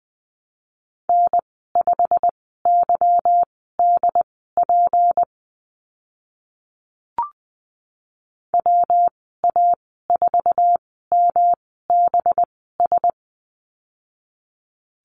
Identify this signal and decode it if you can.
Morse code, heard as a keyed tone, about 20 wpm, spaced as for 15 wpm, 700 Hz. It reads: N5YDP WA4MBS
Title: 186 Lesson 30g - Letter b - Call Signs 20wpm